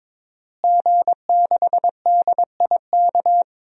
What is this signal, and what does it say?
Morse code, heard as a keyed tone, about 22 wpm, 700 Hz. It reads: G6DIK
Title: DXCC-100 Call Signs - 5 Characters - Repeated 3x - 22wpm